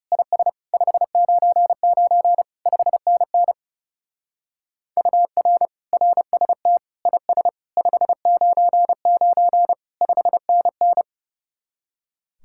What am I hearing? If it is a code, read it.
Morse code, heard as a keyed tone, about 37 wpm, 700 Hz. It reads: IS 599 5NN UR RST IS 599 5NN
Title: Speed-Racing Common QSO Phrases Encore - 30wpm